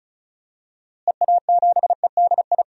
Morse code, heard as a keyed tone, about 34 wpm, 700 Hz. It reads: EA7EDI
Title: DXCC-100 Call Signs - 6 Characters - Repeated 2x - 35wpm